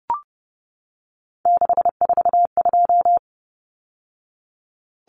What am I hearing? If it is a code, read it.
Morse code, heard as a keyed tone, about 30 wpm, 700 Hz. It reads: 642
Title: Numbers - 3-Digit 30wpm